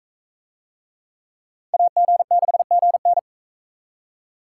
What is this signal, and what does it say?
Morse code, heard as a keyed tone, about 42 wpm, 700 Hz. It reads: AG6GN